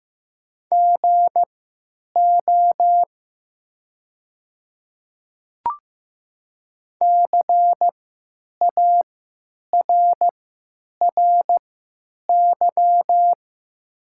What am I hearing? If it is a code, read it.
Morse code, heard as a keyed tone, about 15 wpm, spaced as for 8 wpm, 700 Hz. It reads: GO CARRY
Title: 3x Character Spacing - Top 500 Words - 15wpm